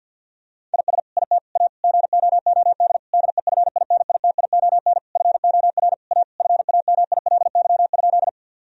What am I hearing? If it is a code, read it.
Morse code, heard as a keyed tone, about 50 wpm, 700 Hz. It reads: IS IT A GOOD DEFINITION FOR A FAMILY?
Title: Sentences from Top 1000 Words - Review Part 4 - 50wpm